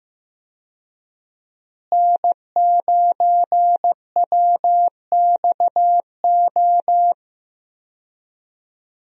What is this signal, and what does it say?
Morse code, heard as a keyed tone, about 15 wpm, 700 Hz. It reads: N9WXO
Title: USA Call Signs - 5 Characters - Repeated 3x - 15wpm